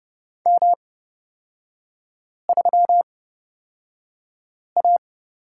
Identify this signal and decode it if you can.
Morse code, heard as a keyed tone, about 31 wpm, spaced as for 5 wpm, 700 Hz. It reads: M3A
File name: Lesson 36e - BT - Three Characters 30wpm